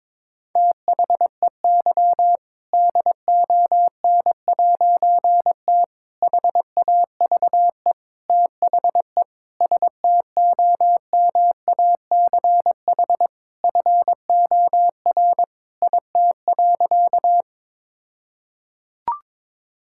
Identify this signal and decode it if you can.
Morse code, heard as a keyed tone, about 22 wpm, 700 Hz. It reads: THEY DON'T HAVE THE STOMACH FOR IT.
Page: morse-code-ninja-practice-22wpm